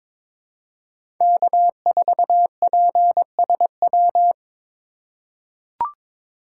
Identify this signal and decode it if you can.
Morse code, heard as a keyed tone, about 22 wpm, 700 Hz. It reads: K4PSW
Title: USA Call Signs - 2x Character Spacing - 22wpm